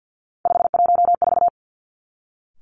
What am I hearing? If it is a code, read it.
Morse code, heard as a keyed tone, about 50 wpm, 700 Hz. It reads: 514
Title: Three Challenging Letter-Number Combinations 50wpm